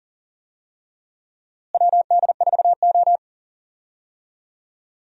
Morse code, heard as a keyed tone, about 40 wpm, 700 Hz. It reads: WD4O
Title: USA Call Signs - 4 Characters - Repeated 3x - 40wpm